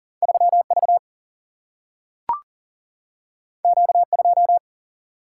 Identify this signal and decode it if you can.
Morse code, heard as a keyed tone, about 40 wpm, 700 Hz. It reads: 3 V Q 2
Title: Single-Letter Single-Number 40wpm